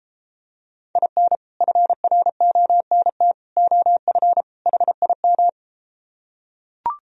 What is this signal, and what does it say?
Morse code, heard as a keyed tone, about 34 wpm, 700 Hz. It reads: IN FRONT OF HIM